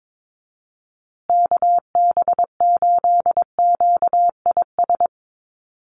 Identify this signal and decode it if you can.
Morse code, heard as a keyed tone, about 22 wpm, 700 Hz. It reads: KB8QIS
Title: USA Call Signs Repeated 3x 22wpm